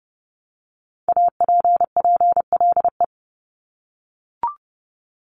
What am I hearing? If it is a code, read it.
Morse code, heard as a keyed tone, about 30 wpm, 700 Hz. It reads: APPLE
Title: Multi-letter Phonemes 30wpm